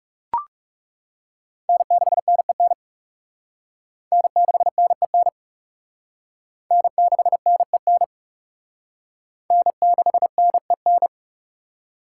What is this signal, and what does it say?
Morse code, heard as a keyed tone, about 36 wpm, 700 Hz. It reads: N6NEN N6NEN N6NEN N6NEN
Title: Speed Racing - USA 1x3 Call Signs 30wpm